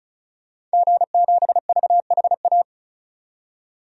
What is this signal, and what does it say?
Morse code, heard as a keyed tone, about 36 wpm, 700 Hz. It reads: G7VHA
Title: DXCC-100 Call Signs - 5 Characters - 35wpm